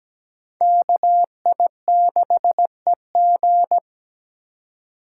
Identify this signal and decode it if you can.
Morse code, heard as a keyed tone, about 17 wpm, 700 Hz. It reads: KI6EG
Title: USA Call Signs - 4x Character Spacing - 17wpm